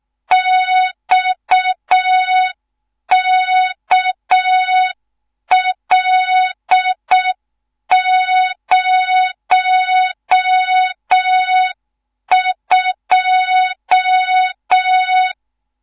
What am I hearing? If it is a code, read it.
Morse code, heard as a keyed tone, about 6 wpm, 3700 Hz. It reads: XKL02
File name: Morze-HA2MN / hang / 14 280 HA2MN